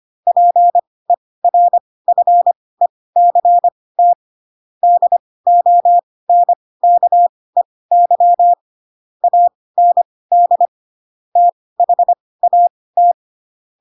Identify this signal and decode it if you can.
Morse code, heard as a keyed tone, about 25 wpm, spaced as for 18 wpm, 700 Hz. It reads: PERFECT DONKEY AND THAT
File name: Pinocchio / Pinocchio ch4 12 25 ik1hge